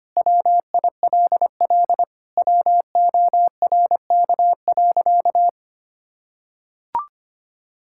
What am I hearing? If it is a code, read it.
Morse code, heard as a keyed tone, about 25 wpm, 700 Hz. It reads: WILL WORK.
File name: practice / 25wpm / Sentences with 4-Word Phrases - 25wpm